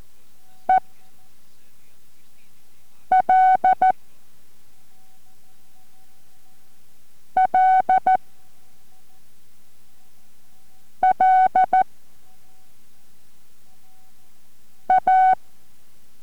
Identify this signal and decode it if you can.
Morse code, heard as a keyed tone, about 14 wpm, 750 Hz. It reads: E L L L A